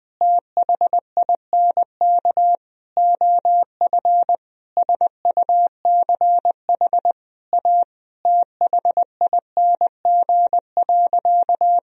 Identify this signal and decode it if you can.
Morse code, heard as a keyed tone, about 20 wpm, 700 Hz. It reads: THINK OF SUCH A THING.
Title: Sentences from Top 1000 Words - Review Part 1 - 20wpm